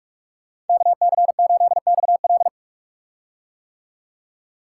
Morse code, heard as a keyed tone, about 44 wpm, 700 Hz. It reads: KC8XL
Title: Code Groups - 5 Letter:Numbers - 45wpm